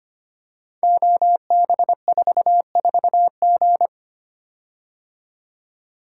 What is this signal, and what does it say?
Morse code, heard as a keyed tone, about 25 wpm, 700 Hz. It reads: OB44G
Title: Code Groups - 5 Letter:Numbers - 25wpm